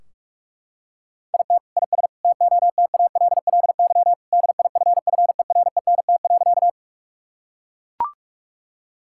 Morse code, heard as a keyed tone, about 44 wpm, 700 Hz. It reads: IT IS TOTALLY DIFFERENT.
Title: Sentences from Top 1600 Words 45wpm